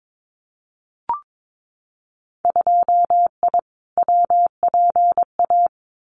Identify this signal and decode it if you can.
Morse code, heard as a keyed tone, about 22 wpm, 700 Hz. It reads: 2I WPA
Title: Winter Field Day Exchanges 22wpm